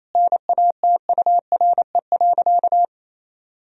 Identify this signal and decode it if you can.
Morse code, heard as a keyed tone, about 27 wpm, 700 Hz. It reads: NATURE.
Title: Sentences from Top 1000 Words - Review Part 3 - 28wpm